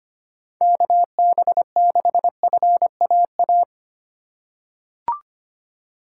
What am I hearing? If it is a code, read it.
Morse code, heard as a keyed tone, about 25 wpm, 700 Hz. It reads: KB6FAA
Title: USA Call Signs - 3x Character Spacing - 25wpm